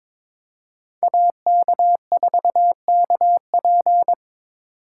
Morse code, heard as a keyed tone, about 25 wpm, 700 Hz. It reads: AK4KP
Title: Speed Racing - USA 2x2 Call Signs 22wpm